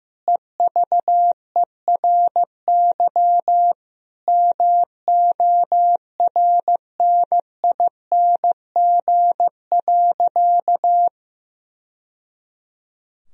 Morse code, heard as a keyed tone, about 15 wpm, 700 Hz. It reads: EVERY MORNING.
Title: Sentences from Top 1000 Words - Review Part 2 - 15wpm